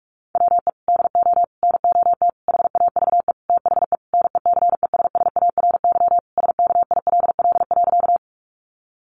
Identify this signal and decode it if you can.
Morse code, heard as a keyed tone, about 44 wpm, 700 Hz. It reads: WE DO NOT HAVE THE NECESSARY SKILL.
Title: Sentences from Top 1200 Words 45wpm